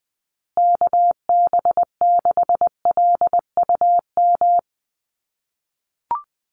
Morse code, heard as a keyed tone, about 20 wpm, 700 Hz. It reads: KB6LUM
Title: USA Call Signs Repeated 2x 20wpm